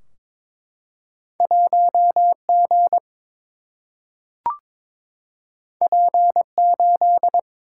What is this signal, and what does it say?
Morse code, heard as a keyed tone, about 22 wpm, 700 Hz. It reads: 1 G P 8
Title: Single-Letter Single-Number 22wpm